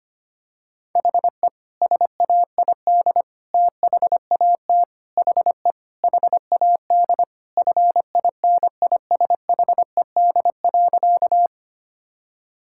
Morse code, heard as a keyed tone, about 25 wpm, 700 Hz. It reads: HE SAID THAT HE HAD FINISHED.
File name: practice / 25wpm / Sentences with 3-Word Phrases - 25wpm